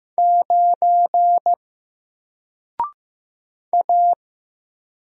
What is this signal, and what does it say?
Morse code, heard as a keyed tone, about 15 wpm, 700 Hz. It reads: 9 A
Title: Single Letter-Number Alt 15wpm